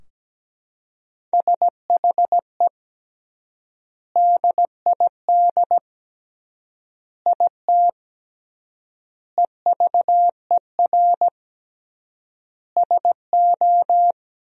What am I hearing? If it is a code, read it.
Morse code, heard as a keyed tone, about 17 wpm, 700 Hz. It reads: SHE DID IT EVER SO
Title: 3x Word-Spacing - Sentences from Top 400 Words - 17wpm